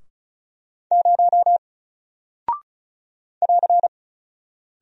Morse code, heard as a keyed tone, about 36 wpm, 700 Hz. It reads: 0 <AR>
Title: Single Character (Letter:Number:Punctuation:Prosign) Alt 35wpm